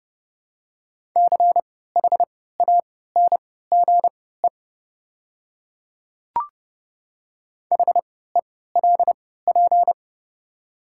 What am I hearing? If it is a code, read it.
Morse code, heard as a keyed tone, about 30 wpm, spaced as for 17 wpm, 700 Hz. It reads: CHANGE HELP
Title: 3x Character Spacing - Top 200 Words - 30wpm